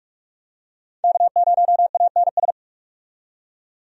Morse code, heard as a keyed tone, about 43 wpm, 700 Hz. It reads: K0ANS
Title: USA Call Signs - 5 Characters - 45wpm